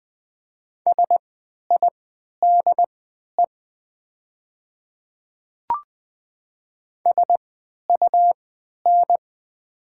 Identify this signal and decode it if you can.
Morse code, heard as a keyed tone, about 20 wpm, spaced as for 11 wpm, 700 Hz. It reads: SIDE SUN
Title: 3x Character Spacing - Top 300 Words - 20wpm